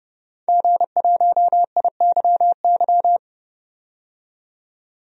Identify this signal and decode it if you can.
Morse code, heard as a keyed tone, about 29 wpm, 700 Hz. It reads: G1IYY
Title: DXCC-100 Call Signs - 5 Characters - 30wpm